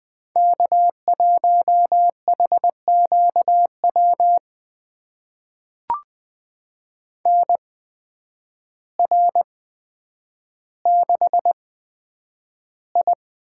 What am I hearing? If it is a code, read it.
Morse code, heard as a keyed tone, about 20 wpm, spaced as for 9 wpm, 700 Hz. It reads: K1HQW NR6I
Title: USA Call Signs - 8x Character Spacing - 20wpm